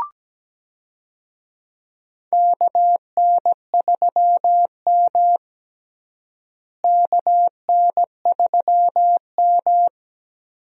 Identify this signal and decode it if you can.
Morse code, heard as a keyed tone, about 17 wpm, 700 Hz. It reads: KN3M KN3M
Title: USA Call Signs - 4 Characters - Repeated 2x - 17wpm